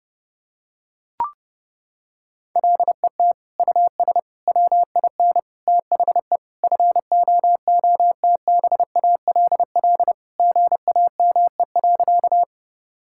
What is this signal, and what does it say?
Morse code, heard as a keyed tone, about 30 wpm, 700 Hz. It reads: LET US WIN THE FOOTBALL GAME.
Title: Sentences from Top 1600 Words 30wpm